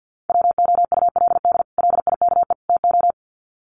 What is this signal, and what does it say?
Morse code, heard as a keyed tone, about 50 wpm, 700 Hz. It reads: WOULD LIKE TO
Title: Sets of 3 Words 50wpm